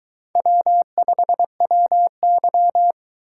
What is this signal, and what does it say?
Morse code, heard as a keyed tone, about 21 wpm, 700 Hz. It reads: W5WY